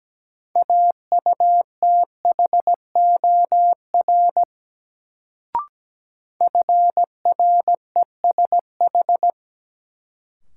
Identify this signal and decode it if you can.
Morse code, heard as a keyed tone, about 17 wpm, 700 Hz. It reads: AUTHOR FRESH
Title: Sentences from Top 1200 Words - New Word Drill - 17wpm